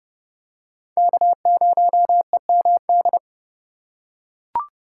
Morse code, heard as a keyed tone, about 26 wpm, 700 Hz. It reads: K0EMD